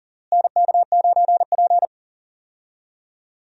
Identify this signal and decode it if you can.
Morse code, heard as a keyed tone, about 40 wpm, 700 Hz. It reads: NK9P
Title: USA Call Signs - 4 Characters - Repeated 2x - 40wpm